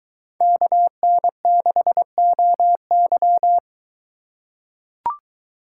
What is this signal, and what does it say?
Morse code, heard as a keyed tone, about 21 wpm, 700 Hz. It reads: KN6OY